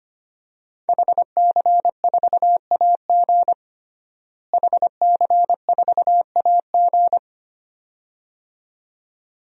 Morse code, heard as a keyed tone, about 25 wpm, 700 Hz. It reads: HC4AG HC4AG
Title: DXCC-100 Call Signs - 5 Characters - Repeated 2x - 25wpm